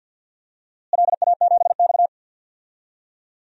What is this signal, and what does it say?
Morse code, heard as a keyed tone, about 51 wpm, 700 Hz. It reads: LA7X
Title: DXCC-100 Call Signs - 4 Characters - Repeated 3x - 50wpm